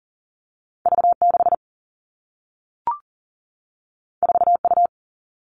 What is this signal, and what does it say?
Morse code, heard as a keyed tone, about 42 wpm, 700 Hz. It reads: V6 4U
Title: All Cut Numbers - 2-Digit 40wpm